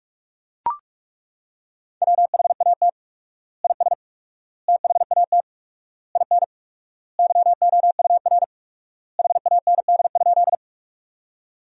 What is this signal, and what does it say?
Morse code, heard as a keyed tone, about 41 wpm, 700 Hz. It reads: WHAT IS THAT IN YOUR HAND?
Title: 4x Word-Spacing - Sentences from Top 200 Words - 45wpm